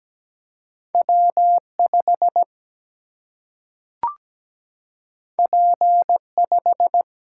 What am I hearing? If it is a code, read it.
Morse code, heard as a keyed tone, about 17 wpm, 700 Hz. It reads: W 5 P 5
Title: Single-Letter Single-Number 17wpm